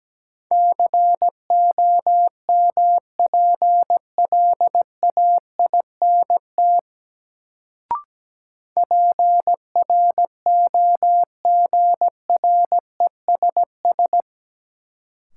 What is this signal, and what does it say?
Morse code, heard as a keyed tone, about 17 wpm, 700 Hz. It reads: COMPLAINT PROGRESS
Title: Sentences from Top 1400 Words - New Word Drill - 17wpm